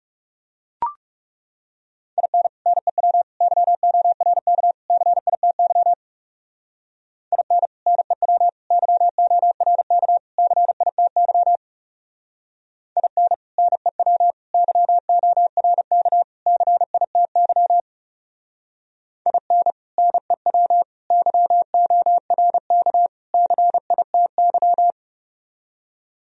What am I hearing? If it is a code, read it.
Morse code, heard as a keyed tone, about 37 wpm, 700 Hz. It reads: IN NEW YORK CITY IN NEW YORK CITY IN NEW YORK CITY IN NEW YORK CITY
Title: Speed-Racing - Sets of 4 Words 30wpm